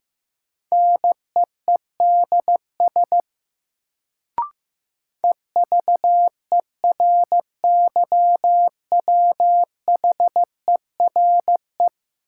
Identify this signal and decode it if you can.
Morse code, heard as a keyed tone, about 15 wpm, 700 Hz. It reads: NEEDS EVERYWHERE
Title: Sentences from Top 1600 Words - New Word Drill Alt - 15wpm